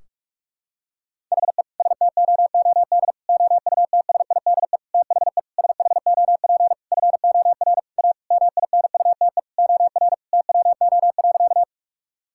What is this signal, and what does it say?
Morse code, heard as a keyed tone, about 46 wpm, 700 Hz. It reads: HE STOOD OUTSIDE THE SHOP FOR A MINUTE OR TWO.
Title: Sentences from Top 1600 Words 45wpm